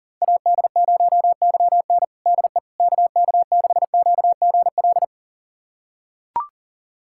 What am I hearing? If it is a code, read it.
Morse code, heard as a keyed tone, about 40 wpm, 700 Hz. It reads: AD0YN DE KK6QGL
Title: USA Callsign DE Callsign 40wpm